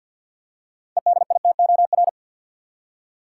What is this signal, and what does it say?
Morse code, heard as a keyed tone, about 46 wpm, 700 Hz. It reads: EDITOR
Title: Six Letter Words - 50wpm